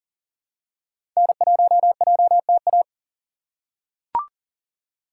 Morse code, heard as a keyed tone, about 40 wpm, 700 Hz. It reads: N1JTA
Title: USA Call Signs - 5 Characters - 6x Character Spacing - 40wpm